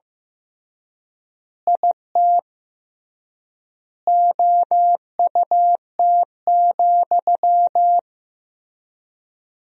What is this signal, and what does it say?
Morse code, heard as a keyed tone, about 15 wpm, 700 Hz. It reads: IT OUT,
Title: 3x Word-Spacing - Sentences from Top 300 Words - 15wpm